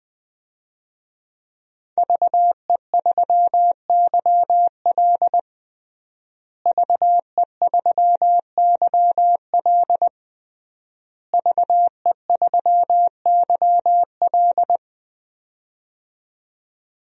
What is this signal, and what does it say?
Morse code, heard as a keyed tone, about 21 wpm, 700 Hz. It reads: VE3YL VE3YL VE3YL
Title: DXCC-100 Call Signs - 5 Characters - Repeated 3x - 20wpm